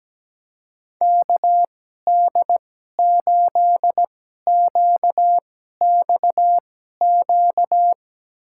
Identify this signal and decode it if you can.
Morse code, heard as a keyed tone, about 17 wpm, spaced as for 12 wpm, 700 Hz. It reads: KD8QXQ
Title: USA Call Signs - 6 Characters - 2x Character Spacing - 17wpm